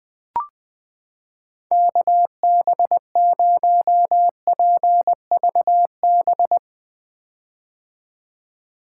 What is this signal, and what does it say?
Morse code, heard as a keyed tone, about 21 wpm, 700 Hz. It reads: KB0PVB
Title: 2023 POTA Activator Call Signs - 20wpm